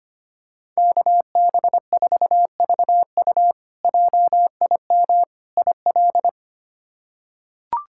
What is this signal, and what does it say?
Morse code, heard as a keyed tone, about 25 wpm, 700 Hz. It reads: KB4VU JIM IL
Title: K1USN SST QSO Practice - 25wpm